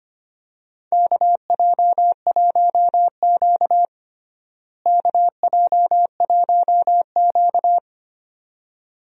Morse code, heard as a keyed tone, about 25 wpm, 700 Hz. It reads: KJ1Q KJ1Q
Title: USA Call Signs - 4 Characters - Repeated 2x - 25wpm